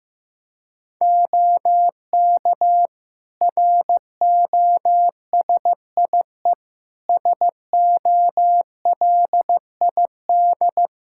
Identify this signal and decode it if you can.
Morse code, heard as a keyed tone, about 15 wpm, 700 Hz. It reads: OK ROSIE SOLID
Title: Practice QSO - Exchange 3 - More Info - 15wpm